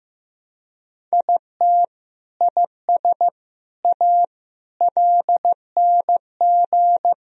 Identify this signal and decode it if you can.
Morse code, heard as a keyed tone, about 15 wpm, 700 Hz. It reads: IT IS A LNG
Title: Sentences from Top 300 Words and Abbreviations 15wpm